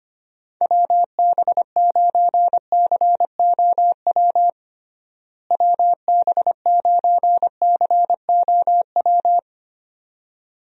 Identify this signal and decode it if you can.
Morse code, heard as a keyed tone, about 25 wpm, 700 Hz. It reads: WB9COW WB9COW
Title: USA Call Signs - 6 Characters - Repeated 2x - 25wpm